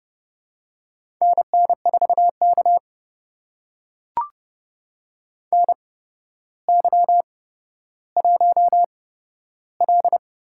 Morse code, heard as a keyed tone, about 30 wpm, spaced as for 13 wpm, 700 Hz. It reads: NN4K NY1L
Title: USA Call Signs - 4 Characters - 8x Character Spacing - 30wpm